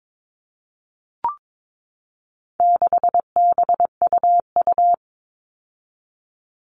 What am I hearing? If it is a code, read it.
Morse code, heard as a keyed tone, about 22 wpm, 700 Hz. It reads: 6BUU
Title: All Cut Numbers - 4-Digit 22wpm